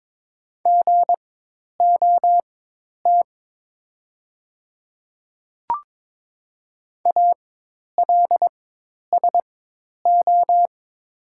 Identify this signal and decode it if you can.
Morse code, heard as a keyed tone, about 22 wpm, spaced as for 10 wpm, 700 Hz. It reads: GOT ALSO
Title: 4x Character Spacing - Top 500 Words - 22wpm